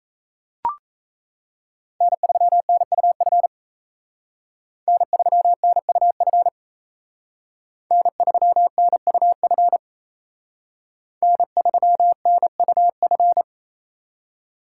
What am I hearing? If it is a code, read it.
Morse code, heard as a keyed tone, about 34 wpm, 700 Hz. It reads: N3NUF N3NUF N3NUF N3NUF
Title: Speed Racing - USA 1x3 Call Signs 28wpm